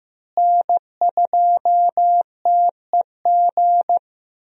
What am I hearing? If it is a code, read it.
Morse code, heard as a keyed tone, about 15 wpm, 700 Hz. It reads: N2TEG
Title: USA Call Signs - 5 Characters - Repeated 2x - 15wpm